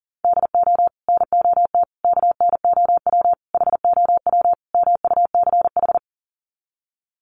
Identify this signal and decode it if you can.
Morse code, heard as a keyed tone, about 40 wpm, 700 Hz. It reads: DO NOT KNOW HOW MUCH
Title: Sets of 5 Words Encore 40wpm